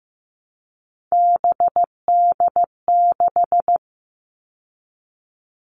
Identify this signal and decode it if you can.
Morse code, heard as a keyed tone, about 15 wpm, 700 Hz. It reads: BD6
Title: All Cut Numbers - 3-Digit 15wpm